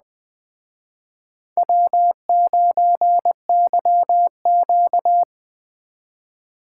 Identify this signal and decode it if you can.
Morse code, heard as a keyed tone, about 20 wpm, 700 Hz. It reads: W9YQ